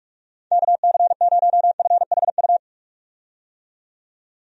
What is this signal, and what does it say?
Morse code, heard as a keyed tone, about 45 wpm, 700 Hz. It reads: KC0FHU
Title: USA Call Signs - 6 Characters - 45wpm